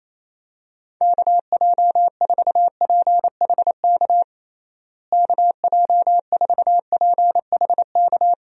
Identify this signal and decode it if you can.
Morse code, heard as a keyed tone, about 28 wpm, 700 Hz. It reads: KJ4PHK KJ4PHK
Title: USA Call Signs - 6 Characters - Repeated 3x - 28wpm